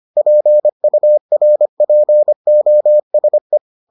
Morse code, heard as a keyed tone, about 25 wpm, 600 Hz. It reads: PURPOSE